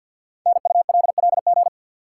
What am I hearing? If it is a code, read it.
Morse code, heard as a keyed tone, about 52 wpm, 700 Hz. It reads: NULLG
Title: Code Groups - 5 Letter:Numbers - 50wpm